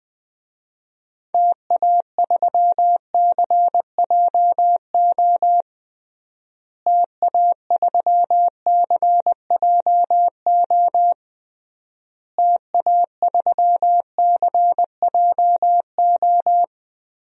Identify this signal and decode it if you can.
Morse code, heard as a keyed tone, about 20 wpm, 700 Hz. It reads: TA3CJO TA3CJO TA3CJO
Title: DXCC-100 Call Signs - 6 Characters - Repeated 3x - 20wpm